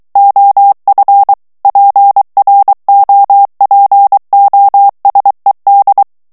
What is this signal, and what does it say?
Morse code, heard as a keyed tone, about 23 wpm, 800 Hz. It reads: OF PROPOSED
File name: ANGLAIS texte clair 3 vitesse 22 mots minute